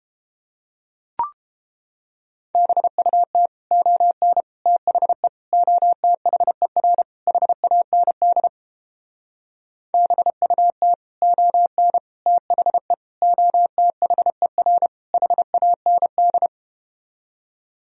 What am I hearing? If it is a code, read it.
Morse code, heard as a keyed tone, about 33 wpm, 700 Hz. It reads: BUT ON THE OTHER HAND BUT ON THE OTHER HAND